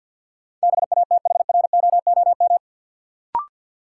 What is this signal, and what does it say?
Morse code, heard as a keyed tone, about 53 wpm, 700 Hz. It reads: BATHROOM
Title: Sets of 5 Words Encore - New Word Drill - 50wpm